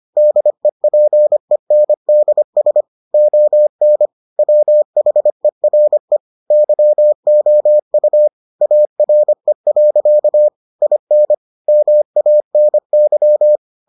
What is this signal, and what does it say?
Morse code, heard as a keyed tone, about 25 wpm, 600 Hz. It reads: DEPENDS ON WHERE YOU ARE. IN MANY